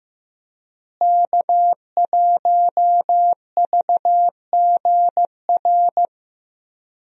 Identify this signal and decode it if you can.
Morse code, heard as a keyed tone, about 15 wpm, 700 Hz. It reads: K1VGR
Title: USA Call Signs - 5 Characters - Repeated 2x - 15wpm